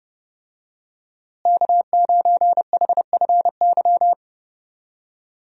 Morse code, heard as a keyed tone, about 31 wpm, 700 Hz. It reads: K9HFY
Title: USA Call Signs - 5 Characters - Repeated 2x - 30wpm